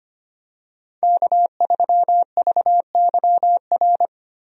Morse code, heard as a keyed tone, about 25 wpm, 700 Hz. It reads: K3VYR
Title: USA Call Signs - 5 Characters - Repeated 3x - 25wpm